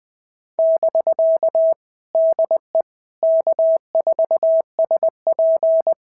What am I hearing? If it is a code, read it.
Morse code, heard as a keyed tone, about 20 wpm, 650 Hz. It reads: <BK> DE K4SP
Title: POTA QSO Practice - 20wpm